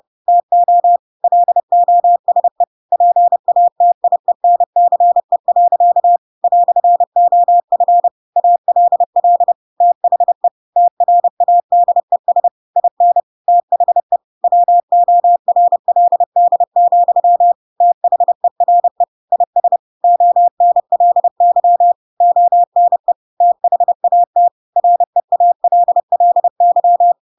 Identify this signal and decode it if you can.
Morse code, heard as a keyed tone, about 30 wpm, 700 Hz. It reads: TO LOSE PATIENCE. "OF ALL THE TRADES IN THE WORLD, THERE IS ONLY ONE THAT REALLY
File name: Pinocchio / Pinocchio ch4 30 30 ik1hge